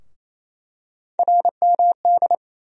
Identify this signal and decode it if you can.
Morse code, heard as a keyed tone, about 28 wpm, 700 Hz. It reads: RMD
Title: USA Call Sign Postfixes - 3 Letters 28wpm